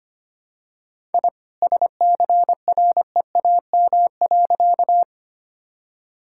Morse code, heard as a keyed tone, about 25 wpm, 700 Hz. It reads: I SCREAM.
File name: practice / 25wpm / All-Age Appropriate Jokes - 25wpm